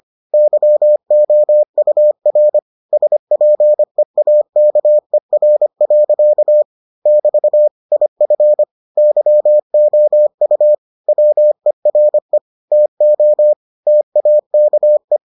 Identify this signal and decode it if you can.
Morse code, heard as a keyed tone, about 25 wpm, 600 Hz. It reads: YOUR SPEAKER. <BT> IF YOU WERE TO TAKE